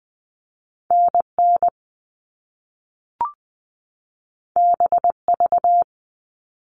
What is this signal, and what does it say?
Morse code, heard as a keyed tone, about 20 wpm, 700 Hz. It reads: N N B V
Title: All Cut Numbers - 2-Digit 20wpm